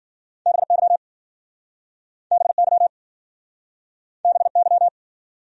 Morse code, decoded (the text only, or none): BY BY BY